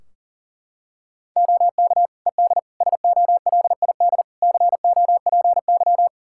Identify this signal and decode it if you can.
Morse code, heard as a keyed tone, about 40 wpm, 700 Hz. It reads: OK ED SOLID COPY
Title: K1USN SST QSO Practice 40wpm